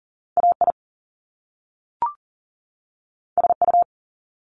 Morse code, heard as a keyed tone, about 40 wpm, 700 Hz. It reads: AI SU
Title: Top Two Letter Combinations in Words 40wpm